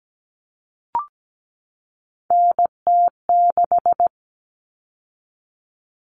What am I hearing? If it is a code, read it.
Morse code, heard as a keyed tone, about 17 wpm, 700 Hz. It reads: NT6
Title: All Cut Numbers - 3-Digit 17wpm